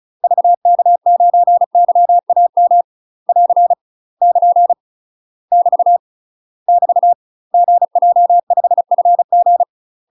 Morse code, heard as a keyed tone, about 34 wpm, 700 Hz. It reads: VK9YAM <AR> <KN> <BT> <BT> GJ5FG
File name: UK MASTER QSOs TRAINING / 35WPM / UK MASTER QSOs 0000